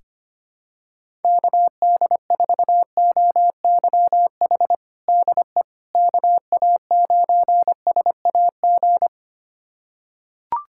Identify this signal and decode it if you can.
Morse code, heard as a keyed tone, about 25 wpm, 700 Hz. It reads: KD4OYH DE KA9SAG
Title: USA Callsign DE Callsign - 6 Characters - 25wpm